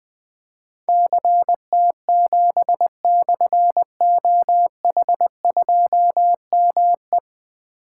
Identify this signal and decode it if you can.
Morse code, heard as a keyed tone, about 20 wpm, 700 Hz. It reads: CT7/OH2ME
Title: Difficult Call Signs 20wpm